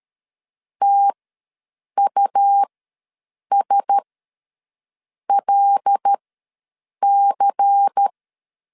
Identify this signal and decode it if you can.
Morse code, heard as a keyed tone, about 13 wpm, spaced as for 7 wpm, 800 Hz. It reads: TUSLC
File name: LEZIONI / 5 LBVC / 5 LBVC ws 12wpm 3159